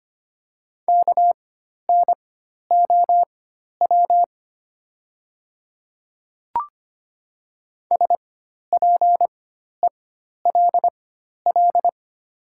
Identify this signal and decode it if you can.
Morse code, heard as a keyed tone, about 25 wpm, spaced as for 12 wpm, 700 Hz. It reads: KNOW SPELL
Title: 4x Character Spacing - Top 300 Words - 25wpm